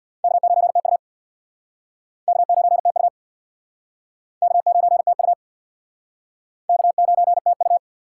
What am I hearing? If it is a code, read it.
Morse code, heard as a keyed tone, about 65 wpm, 700 Hz. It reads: K9TU K9TU K9TU K9TU